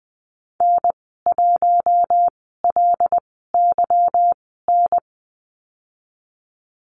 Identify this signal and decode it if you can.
Morse code, heard as a keyed tone, about 20 wpm, spaced as for 15 wpm, 700 Hz. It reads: N1LYN
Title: 147 Lesson 24g - Letter y - Call Signs 20wpm